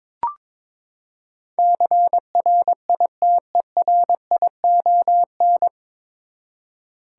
Morse code, heard as a keyed tone, about 22 wpm, 700 Hz. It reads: CRITERION
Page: Morse Code Ninja practice 22wpm